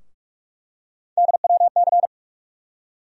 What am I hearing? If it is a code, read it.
Morse code, heard as a keyed tone, about 42 wpm, 700 Hz. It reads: DWC
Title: USA Call Sign Postfixes - 3 Letters 45wpm